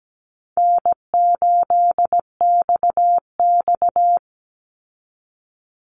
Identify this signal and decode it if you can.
Morse code, heard as a keyed tone, about 17 wpm, 700 Hz. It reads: N8XX
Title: Easy Call Signs Repeated 2x - 17wpm